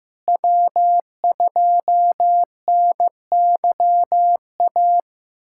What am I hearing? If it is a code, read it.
Morse code, heard as a keyed tone, about 15 wpm, 700 Hz. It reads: W2NYA
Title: USA Callsign DE Callsign - 5 Characters - 15wpm